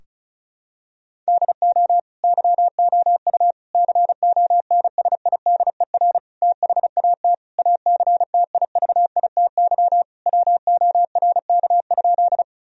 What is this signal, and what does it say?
Morse code, heard as a keyed tone, about 36 wpm, 700 Hz. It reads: DO YOU CONSIDER THAT ACTIVITY WORK?
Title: Sentences from Top 1000 Words - Review Part 4 - 35wpm